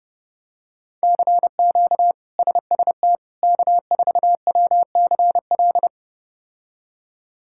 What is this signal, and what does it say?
Morse code, heard as a keyed tone, about 30 wpm, 700 Hz. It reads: CQ SST K4WCL
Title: K1USN SST QSO Practice - 30wpm